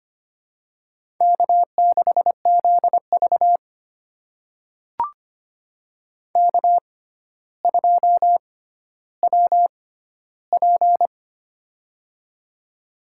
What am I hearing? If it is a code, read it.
Morse code, heard as a keyed tone, about 25 wpm, spaced as for 13 wpm, 700 Hz. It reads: K6ZV K2WP
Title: USA Call Signs - 4 Characters - 6x Character Spacing - 25wpm